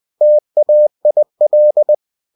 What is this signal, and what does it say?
Morse code, heard as a keyed tone, about 20 wpm, 600 Hz. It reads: TAIL
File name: cw-books / winnie-the-pooh / winnie-20wpm-page-0087